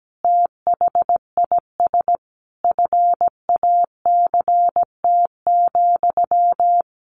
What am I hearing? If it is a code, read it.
Morse code, heard as a keyed tone, about 17 wpm, 700 Hz. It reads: THIS FACT,
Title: Sentences from Top 1200 Words 17wpm